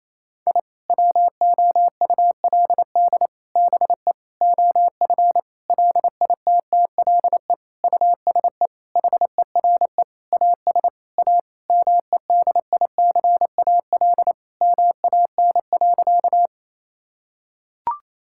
Morse code, heard as a keyed tone, about 28 wpm, 700 Hz. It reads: I WOULD BE OF LITTLE USE HERE AS A MEDICAL MAN.
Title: Sentences from Top 1000 Words - Review Part 5 - 28wpm